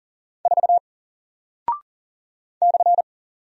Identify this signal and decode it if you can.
Morse code, heard as a keyed tone, about 40 wpm, 700 Hz. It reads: 4 /
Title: Single Character (Letter:Number:Punctuation:Prosign) Alt 40wpm